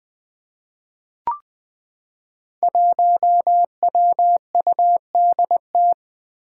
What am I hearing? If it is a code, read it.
Morse code, heard as a keyed tone, about 21 wpm, 700 Hz. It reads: 1WUDT
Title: Five Challenging Letter-Number Combinations 20wpm